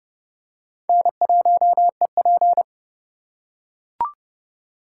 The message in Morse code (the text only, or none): N1EP